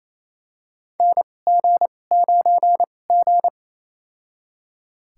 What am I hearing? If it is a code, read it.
Morse code, heard as a keyed tone, about 28 wpm, spaced as for 20 wpm, 700 Hz. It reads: NG9G
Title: USA Call Signs - 2x Character Spacing - 28wpm